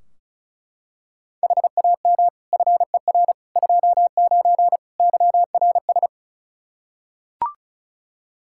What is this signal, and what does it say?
Morse code, heard as a keyed tone, about 36 wpm, 700 Hz. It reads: HAM FER 29 YRS
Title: Practice QSO - Exchange 3 - More Info - 35wpm